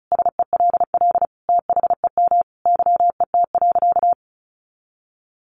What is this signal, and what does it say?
Morse code, heard as a keyed tone, about 34 wpm, 700 Hz. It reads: SELL THEM YET.
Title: Sentences from Top 1100 Words 35wpm